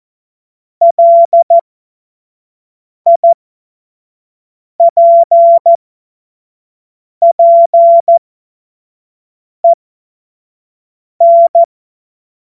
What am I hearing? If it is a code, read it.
Morse code, heard as a keyed tone, about 14 wpm, spaced as for 5 wpm, 680 Hz. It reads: LIPPEN